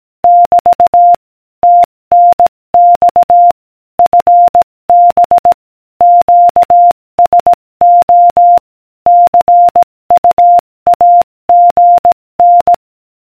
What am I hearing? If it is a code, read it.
Morse code, heard as a keyed tone, about 18 wpm, 700 Hz. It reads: <BT> TNX FB QSO CUAGN